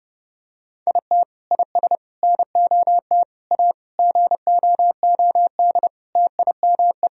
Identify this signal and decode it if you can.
Morse code, heard as a keyed tone, about 32 wpm, 700 Hz. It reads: IT IS NOT A GOOD TIME
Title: Sentences from Top 1000 Words - Review Part 2 - 30wpm